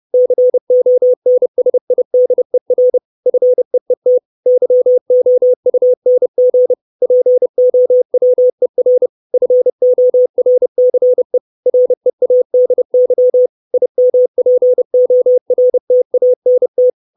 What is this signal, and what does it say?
Morse code, heard as a keyed tone, about 30 wpm, 500 Hz. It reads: CONSIDER FEET YOUNG POWER FORCE READY IMPORTANT
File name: 632 Words 30WPM